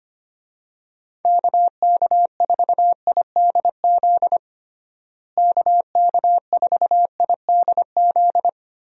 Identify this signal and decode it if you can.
Morse code, heard as a keyed tone, about 25 wpm, 700 Hz. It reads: KK4IDZ KK4IDZ
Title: USA Call Signs - 6 Characters - Repeated 2x - 25wpm